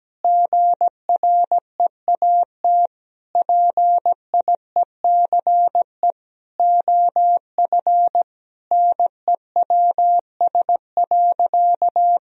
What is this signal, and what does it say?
Morse code, heard as a keyed tone, about 17 wpm, 700 Hz. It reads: GREAT PIECE OF NEWS.
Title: Sentences from Top 1000 Words - Review Part 2 - 17wpm